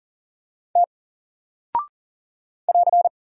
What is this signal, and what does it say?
Morse code, heard as a keyed tone, about 40 wpm, 700 Hz. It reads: T <AR>
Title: Single Character (Letter:Number:Punctuation:Prosign) Alt 40wpm